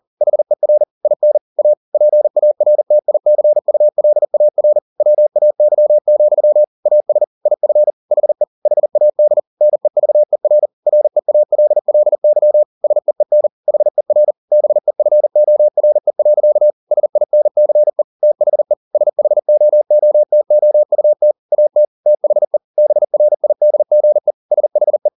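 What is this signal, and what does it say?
Morse code, heard as a keyed tone, about 40 wpm, 600 Hz. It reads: HER IN A PARTICULAR WAY, AS IF HE HAD NEVER REALLY SEEN HER BEFORE. SINCE THE SHOOTOUT AT THE BRIDGE SHE
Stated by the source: Morse code at 40 wpm, practice text, CW med Gnister.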